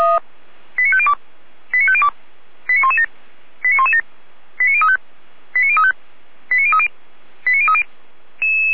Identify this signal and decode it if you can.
Five-tone selective call (ZVEI), heard as quick runs of five tones, tones of about 70 ms. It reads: 89502 89502 89117 89117 89035 89035 89030 89030